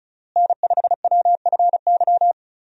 Morse code, heard as a keyed tone, about 36 wpm, 700 Hz. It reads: N5WFY
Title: USA Call Signs - 5 Characters - Repeated 3x - 35wpm